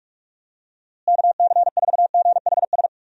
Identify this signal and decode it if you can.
Morse code, heard as a keyed tone, about 46 wpm, 700 Hz. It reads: KC4GHS
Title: USA Call Signs - 6 Characters - Repeated 2x - 45wpm